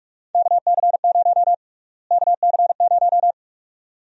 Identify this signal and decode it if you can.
Morse code, heard as a keyed tone, about 46 wpm, 700 Hz. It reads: KC0 KC0
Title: USA Call Sign Prefixes - Repeated 2x - 45wpm